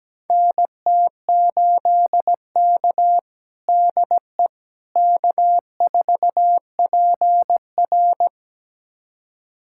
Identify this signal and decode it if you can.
Morse code, heard as a keyed tone, about 17 wpm, 700 Hz. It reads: NT8K DE K4PR
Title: USA Callsign DE Callsign - 4 Characters - 17wpm